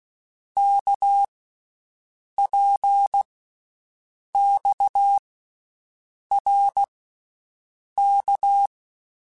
Morse code, heard as a keyed tone, about 16 wpm, 800 Hz. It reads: K P X R K
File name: Basic Letter Groups By7Groups / RKPX / RKPX@13